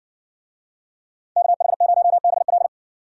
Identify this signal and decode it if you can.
Morse code, heard as a keyed tone, about 55 wpm, 700 Hz. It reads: KH0BL